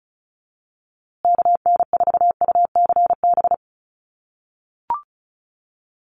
Morse code, heard as a keyed tone, about 34 wpm, 700 Hz. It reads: KN4UCB
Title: USA Call Signs Repeated 2x 35wpm